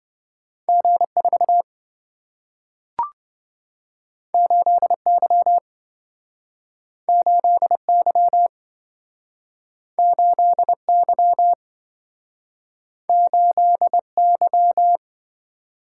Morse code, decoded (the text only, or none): G4 8Y 8Y 8Y 8Y